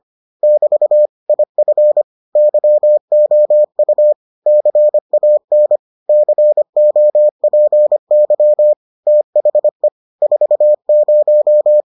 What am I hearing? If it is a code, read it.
Morse code, heard as a keyed tone, about 25 wpm, 600 Hz. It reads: <BT> IF YOU CAN COPY THE 40